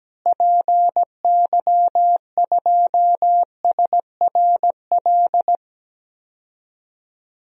Morse code, heard as a keyed tone, about 17 wpm, 700 Hz. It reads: PY2SRL
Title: DXCC-100 Call Signs - 6 Characters - Repeated 2x - 17wpm